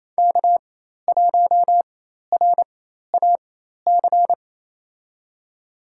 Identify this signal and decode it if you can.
Morse code, heard as a keyed tone, about 28 wpm, spaced as for 13 wpm, 700 Hz. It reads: K1RAC
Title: USA Call Signs - 5 Characters - 4x Character Spacing - 28wpm